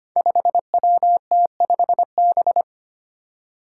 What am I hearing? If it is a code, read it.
Morse code, heard as a keyed tone, about 25 wpm, 700 Hz. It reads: 5WT5B
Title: Five Challenging Letter-Number Combinations 25wpm